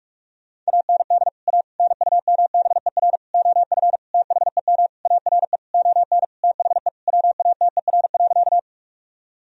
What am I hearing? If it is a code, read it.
Morse code, heard as a keyed tone, about 48 wpm, 700 Hz. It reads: AND A NUMBER OF THEM ARE ON THE WATER.
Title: Sentences from Top 1000 Words - Review Part 1 - 45wpm